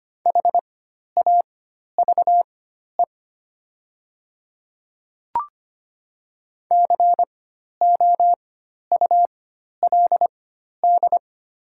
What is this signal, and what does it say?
Morse code, heard as a keyed tone, about 25 wpm, spaced as for 12 wpm, 700 Hz. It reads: HAVE COULD
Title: 4x Character Spacing - Top 100 Words - 25wpm